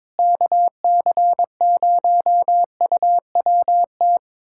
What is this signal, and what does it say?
Morse code, heard as a keyed tone, about 22 wpm, 700 Hz. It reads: KC0UWT
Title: USA Callsign DE Callsign - 6 Characters - 22wpm